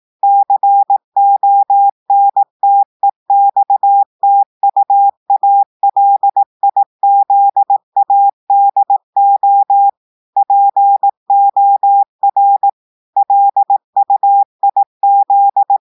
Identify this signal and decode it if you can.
Morse code, heard as a keyed tone, about 18 wpm, 800 Hz. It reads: CONTEXTUALIZADO POR LUIZ
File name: audios / portugais / portugais texte en clair 1 vitesse 18 mots minutes